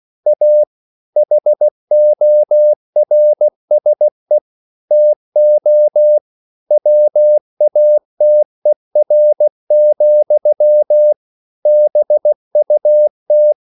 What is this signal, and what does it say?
Morse code, heard as a keyed tone, about 16 wpm, 600 Hz. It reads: A HORSE TO WATER, BUT
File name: CW / Practice / EnglishSayings16